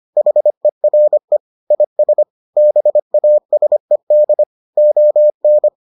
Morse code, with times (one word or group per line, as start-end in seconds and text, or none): 0.2-1.4 HERE
1.7-2.2 IS
2.6-4.4 BASED
4.8-5.7 ON